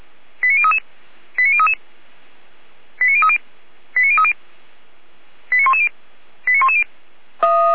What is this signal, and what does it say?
Five-tone selective call (ZVEI), heard as quick runs of five tones, tones of about 70 ms. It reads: 89033 89033 89030 89030 89110 89110
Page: SDR/ZVEI